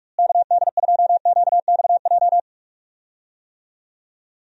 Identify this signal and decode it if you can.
Morse code, heard as a keyed tone, about 46 wpm, 700 Hz. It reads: KD2QXJ
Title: USA Call Signs - 6 Characters - Repeated 2x - 45wpm